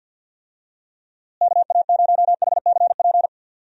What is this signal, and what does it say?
Morse code, heard as a keyed tone, about 51 wpm, 700 Hz. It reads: KA0HCP
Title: USA Call Signs - 6 Characters - Repeated 3x - 50wpm